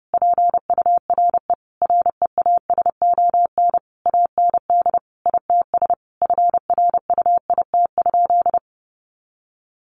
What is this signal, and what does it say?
Morse code, heard as a keyed tone, about 30 wpm, 700 Hz. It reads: PURE REASON AND ITS FRUIT?
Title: Sentences from Top 1500 Words 30wpm